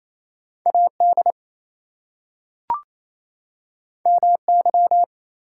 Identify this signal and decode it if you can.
Morse code, heard as a keyed tone, about 28 wpm, 700 Hz. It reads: AD MY
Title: Two Letter Words - 28wpm